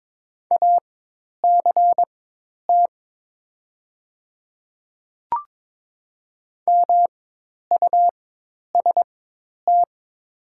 Morse code, heard as a keyed tone, about 22 wpm, spaced as for 10 wpm, 700 Hz. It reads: ACT MUST
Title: 4x Character Spacing - Top 200 Words - 22wpm